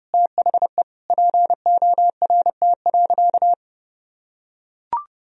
Morse code, heard as a keyed tone, about 32 wpm, 700 Hz. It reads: THE PORT.
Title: Sentences from Top 1000 Words - Review Part 2 - 30wpm